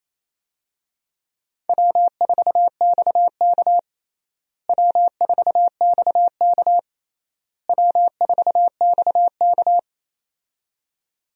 Morse code, heard as a keyed tone, about 28 wpm, 700 Hz. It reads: W4XK W4XK W4XK
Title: USA Call Signs - 4 Characters - Repeated 3x - 28wpm